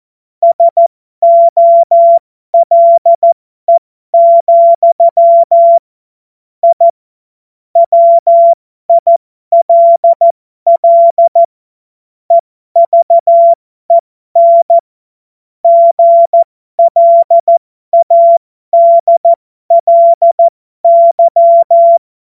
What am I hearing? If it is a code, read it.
Morse code, heard as a keyed tone, about 14 wpm, spaced as for 12 wpm, 680 Hz. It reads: SOLE, I WILL EVEN GLADLY
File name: Morsekurs / Witze